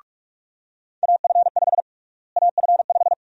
Morse code, heard as a keyed tone, about 46 wpm, 700 Hz. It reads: AF5 AF5
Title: USA Call Sign Prefixes - Repeated 2x - 45wpm